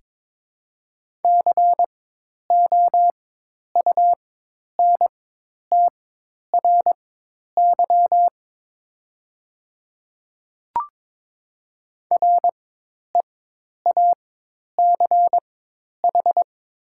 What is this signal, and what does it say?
Morse code, heard as a keyed tone, about 22 wpm, spaced as for 10 wpm, 700 Hz. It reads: COUNTRY REACH
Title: 4x Character Spacing - Top 500 Words - 22wpm